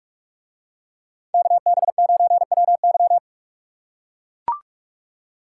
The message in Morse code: KB9WY

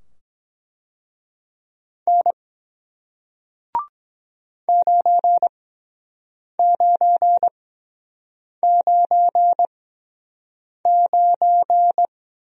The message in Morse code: N 9999

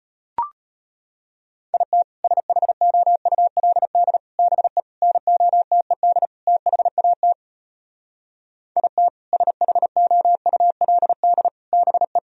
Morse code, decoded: IT SHOULD BE NOTED THAT IT SHOULD BE

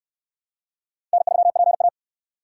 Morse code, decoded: N3PA

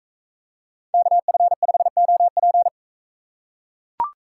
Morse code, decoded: KF5OP

